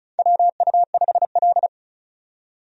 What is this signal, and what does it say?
Morse code, heard as a keyed tone, about 34 wpm, 700 Hz. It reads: WU5L
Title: USA Call Signs - 4 Characters - 35wpm